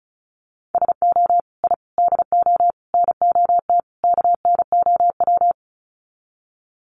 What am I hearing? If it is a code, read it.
Morse code, heard as a keyed tone, about 34 wpm, 700 Hz. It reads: SO I DO NOT KNOW
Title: Sets of 5 Words Encore 35wpm